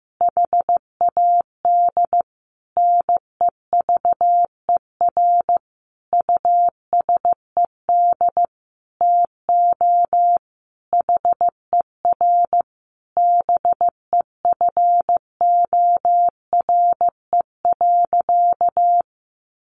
Morse code, heard as a keyed tone, about 15 wpm, 700 Hz. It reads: HAD NEVER USED TO HER BEFORE.
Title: Sentences from Top 600 Words 15wpm